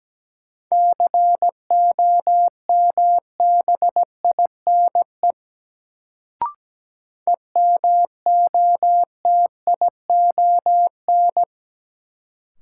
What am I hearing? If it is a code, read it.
Morse code, heard as a keyed tone, about 17 wpm, 700 Hz. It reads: COMBINE EMOTION
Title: Sentences from Top 1600 Words - New Word Drill Alt - 17wpm